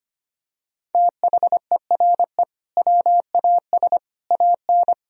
Morse code, heard as a keyed tone, about 25 wpm, 700 Hz. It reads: THERE WAS AN
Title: Sentences from Top 1000 Words - Review Part 3 - 25wpm